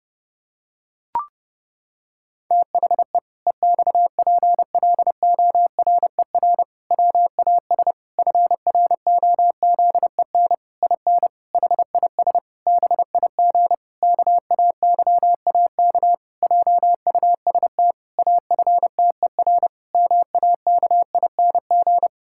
Morse code, heard as a keyed tone, about 30 wpm, 700 Hz. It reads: THE EXPLORER WAS FROZEN IN HIS BIG KAYAK JUST AFTER MAKING
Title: Pangram Challenge v2 - 30wpm